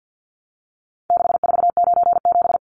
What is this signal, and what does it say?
Morse code, heard as a keyed tone, about 50 wpm, 700 Hz. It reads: 6497
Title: Numbers - 4-Digit 50wpm